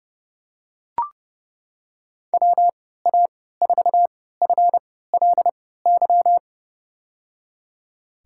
Morse code, heard as a keyed tone, about 30 wpm, spaced as for 17 wpm, 700 Hz. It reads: WA4FLY
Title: USA Call Signs - 6 Characters - 3x Character Spacing - 30wpm